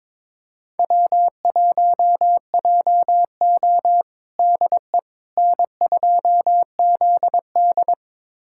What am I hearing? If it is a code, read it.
Morse code, heard as a keyed tone, about 22 wpm, 700 Hz. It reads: W1JO DE N2ZD
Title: USA Callsign DE Callsign - 4 Characters - 22wpm